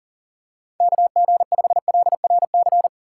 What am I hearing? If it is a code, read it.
Morse code, heard as a keyed tone, about 37 wpm, 700 Hz. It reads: KG5LRC
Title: USA Call Signs - 6 Characters - 2x Character Spacing - 40wpm